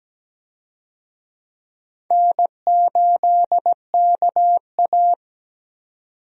Morse code, heard as a keyed tone, about 17 wpm, 700 Hz. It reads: N8KA
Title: USA Call Signs - 4 Characters - 17wpm